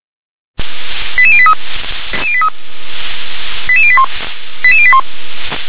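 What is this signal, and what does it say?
Five-tone selective call (ZVEI), heard as quick runs of five tones, tones of about 70 ms. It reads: 88083 88081 88081
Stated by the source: ZVEI five-tone selective call 88702 received by SDR.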